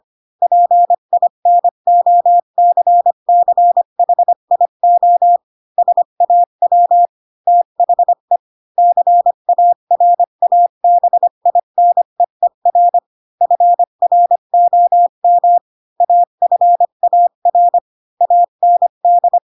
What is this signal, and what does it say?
Morse code, heard as a keyed tone, about 25 wpm, 700 Hz. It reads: PINOCCHIO SAW THE CARABINEER FROM AFAR AND
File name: Pinocchio / Pinocchio ch3 20 25 ik1hge